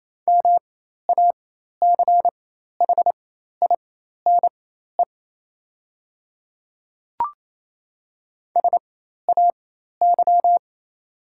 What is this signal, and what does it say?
Morse code, heard as a keyed tone, about 28 wpm, spaced as for 13 wpm, 700 Hz. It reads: MACHINE SAY
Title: 4x Character Spacing - Top 500 Words - 28wpm